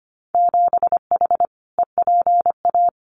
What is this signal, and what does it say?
Morse code, heard as a keyed tone, about 25 wpm, 700 Hz. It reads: 7H EPA
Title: Winter Field Day Exchanges Encore 25wpm